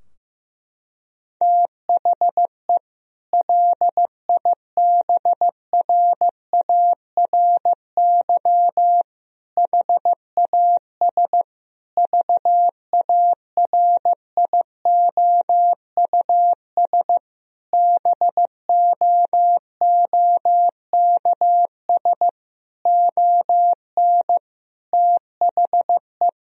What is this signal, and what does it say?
Morse code, heard as a keyed tone, about 15 wpm, 700 Hz. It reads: THE LIBRARY HAS VARIOUS BOOKS ON THE
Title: Sentences from Top 1000 Words - Review Part 4 - 15wpm